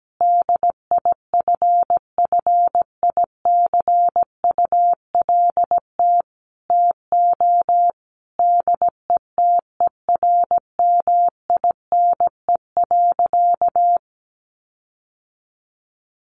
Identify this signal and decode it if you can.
Morse code, heard as a keyed tone, about 17 wpm, 700 Hz. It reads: DIFFICULT TO DETERMINE.
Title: Sentences from Top 1500 Words 17wpm